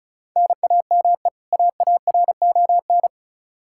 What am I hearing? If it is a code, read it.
Morse code, heard as a keyed tone, about 36 wpm, 700 Hz. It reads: NAME AARON
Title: Practice QSO - Exchange 1 - Beginning - 35wpm